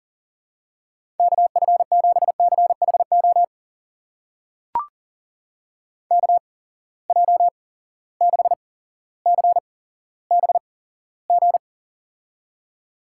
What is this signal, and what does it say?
Morse code, heard as a keyed tone, about 41 wpm, spaced as for 17 wpm, 700 Hz. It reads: KF7CHO KJ6CBG
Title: USA Call Signs - 6 Characters - 8x Character Spacing - 40wpm